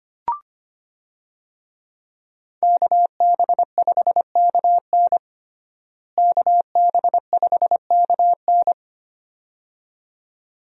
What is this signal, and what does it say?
Morse code, heard as a keyed tone, about 25 wpm, 700 Hz. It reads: KB5KN KB5KN
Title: USA Call Signs - 5 Characters - Repeated 2x - 25wpm